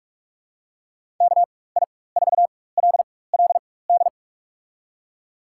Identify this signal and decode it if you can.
Morse code, heard as a keyed tone, about 47 wpm, spaced as for 21 wpm, 700 Hz. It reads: KI4LLD
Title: USA Call Signs - 6 Characters - 4x Character Spacing - 45wpm